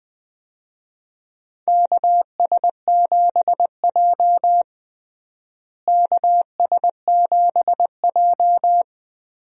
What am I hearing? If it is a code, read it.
Morse code, heard as a keyed tone, about 20 wpm, 700 Hz. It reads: KS7J KS7J
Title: USA Call Signs - 4 Characters - Repeated 2x - 20wpm